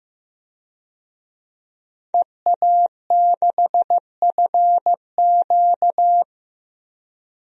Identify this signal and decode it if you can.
Morse code, heard as a keyed tone, about 15 wpm, 700 Hz. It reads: EA6FQ
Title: DXCC-100 Call Signs - 5 Characters - 15wpm